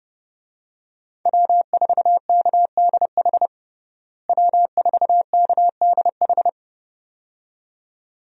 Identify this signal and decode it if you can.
Morse code, heard as a keyed tone, about 30 wpm, 700 Hz. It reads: W4KDH W4KDH
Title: USA Call Signs - 5 Characters - Repeated 2x - 30wpm